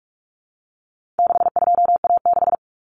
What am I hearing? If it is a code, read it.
Morse code, heard as a keyed tone, about 44 wpm, 700 Hz. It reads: 62A6
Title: Common Cut Numbers - 4-Digit 45wpm